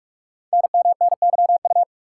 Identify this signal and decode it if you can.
Morse code, heard as a keyed tone, about 44 wpm, 700 Hz. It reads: NMNYU
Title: Code Groups - 5 Letters - 45wpm